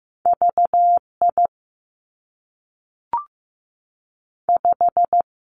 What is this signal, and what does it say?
Morse code, heard as a keyed tone, about 15 wpm, 700 Hz. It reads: V I 5
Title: Two Challenging Letter-Number Combinations 15wpm